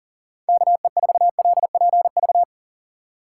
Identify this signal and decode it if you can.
Morse code, heard as a keyed tone, about 40 wpm, 700 Hz. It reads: KE4LPV
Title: USA Call Signs - 6 Characters - 6x Character Spacing - 40wpm